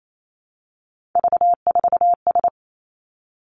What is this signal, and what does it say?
Morse code, heard as a keyed tone, about 28 wpm, 700 Hz. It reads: V4S
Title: Three Challenging Letter-Number Combinations 28wpm